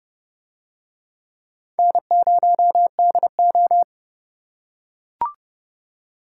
Morse code, heard as a keyed tone, about 29 wpm, 690 Hz. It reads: N0DO